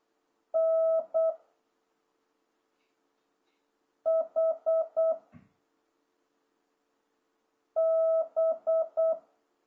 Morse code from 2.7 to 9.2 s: H B